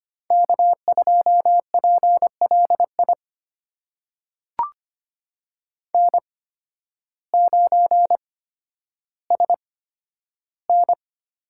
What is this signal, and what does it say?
Morse code, heard as a keyed tone, about 25 wpm, spaced as for 12 wpm, 700 Hz. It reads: K2PLI N9SN
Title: USA Call Signs - 5 Characters - 8x Character Spacing - 25wpm